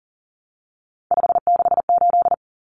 Morse code, heard as a keyed tone, about 40 wpm, 700 Hz. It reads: 568
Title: Common Cut Numbers - 3-Digit 40wpm